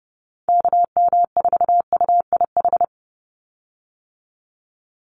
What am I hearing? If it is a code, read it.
Morse code, heard as a keyed tone, about 30 wpm, 700 Hz. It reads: KM4UIH
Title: USA Call Signs 30wpm